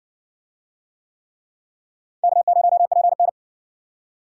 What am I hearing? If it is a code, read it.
Morse code, heard as a keyed tone, about 58 wpm, 700 Hz. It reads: K0PN